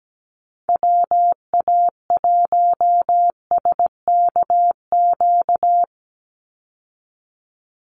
Morse code, heard as a keyed tone, about 17 wpm, 700 Hz. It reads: WA1SKQ
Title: Call Signs Repeated 2x - 17wpm